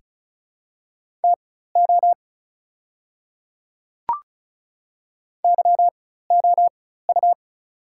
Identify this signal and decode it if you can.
Morse code, heard as a keyed tone, about 36 wpm, spaced as for 16 wpm, 700 Hz. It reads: TO YOU
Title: 4x Character Spacing - Top 500 Words - 35wpm